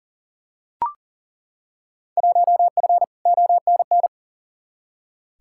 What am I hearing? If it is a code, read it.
Morse code, heard as a keyed tone, about 40 wpm, 700 Hz. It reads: 1F ONN
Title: Field Day Exchanges Encore - 40wpm